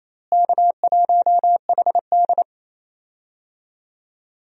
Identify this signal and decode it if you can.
Morse code, heard as a keyed tone, about 28 wpm, 700 Hz. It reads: K1HD
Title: USA Call Signs - 4 Characters - Repeated 3x - 28wpm